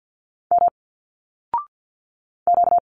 Morse code, heard as a keyed tone, about 50 wpm, 700 Hz. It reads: M Q
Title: Single Letters 50wpm